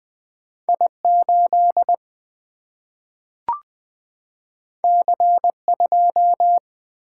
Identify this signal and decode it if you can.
Morse code, heard as a keyed tone, about 20 wpm, 700 Hz. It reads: I8 C2
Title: Single-Letter Single-Number 20wpm